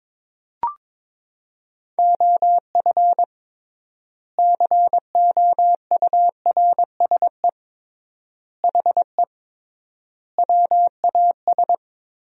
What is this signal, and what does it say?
Morse code, heard as a keyed tone, about 22 wpm, 700 Hz. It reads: OF COURSE HE WAS
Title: 3x Word-Spacing - Sentences from Top 500 Words - 22wpm